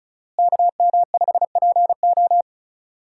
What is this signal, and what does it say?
Morse code, heard as a keyed tone, about 34 wpm, 700 Hz. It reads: KM5PO
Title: USA Call Signs - 5 Characters - Repeated 2x - 35wpm